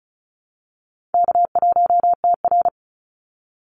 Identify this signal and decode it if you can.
Morse code, heard as a keyed tone, about 34 wpm, 700 Hz. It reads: K1TR
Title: Easy Call Signs Repeated 3x - 35wpm